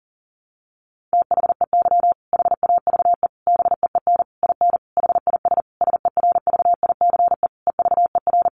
Morse code, heard as a keyed tone, about 40 wpm, 700 Hz. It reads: THEY HAVE BEEN IN HIS SERVICE EVER
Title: Sentences from Top 700 Words 40wpm